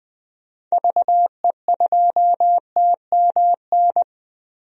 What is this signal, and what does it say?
Morse code, heard as a keyed tone, about 20 wpm, 700 Hz. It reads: VE2TMN
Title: DXCC-100 Call Signs - 6 Characters - Repeated 2x - 20wpm